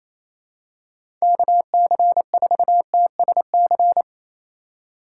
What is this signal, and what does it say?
Morse code, heard as a keyed tone, about 28 wpm, 700 Hz. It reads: KC4TSC
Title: USA Call Signs - 6 Characters - 2x Character Spacing - 28wpm